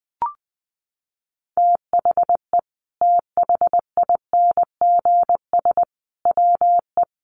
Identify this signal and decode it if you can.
Morse code, heard as a keyed tone, about 20 wpm, 700 Hz. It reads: THE THINGS WE
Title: A Sound Of Thunder 20wpm